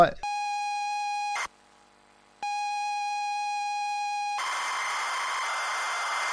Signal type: computer data on cassette